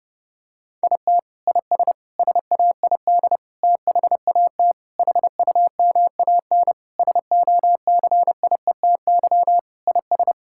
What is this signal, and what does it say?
Morse code, heard as a keyed tone, about 32 wpm, 700 Hz. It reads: IT IS SAID THAT HUMAN SOCIETY IS
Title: Sentences from Top 1000 Words - Review Part 6 - 30wpm